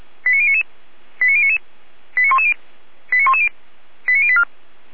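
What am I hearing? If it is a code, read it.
Five-tone selective call (ZVEI), heard as quick runs of five tones, tones of about 70 ms. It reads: 89099 89099 89110 89110 89084
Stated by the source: ZVEI five-tone selective call received by SDR.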